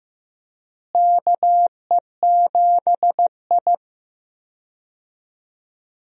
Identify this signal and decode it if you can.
Morse code, heard as a keyed tone, about 15 wpm, 700 Hz. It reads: KE7I
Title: USA Call Signs - 4 Characters - Repeated 2x - 15wpm